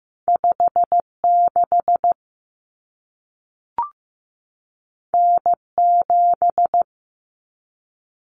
Morse code, heard as a keyed tone, about 15 wpm, 700 Hz. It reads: 56 N7
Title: Common Cut Numbers - 2-Digit 15wpm